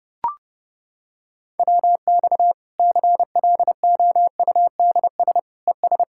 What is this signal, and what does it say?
Morse code, heard as a keyed tone, about 30 wpm, 700 Hz. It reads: WX CLOUDS ES
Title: Practice QSO - Exchange 2 - Info - 30wpm